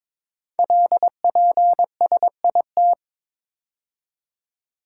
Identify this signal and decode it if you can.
Morse code, heard as a keyed tone, about 22 wpm, 700 Hz. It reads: LPSIT
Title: Code Groups - 5 Letters - 22wpm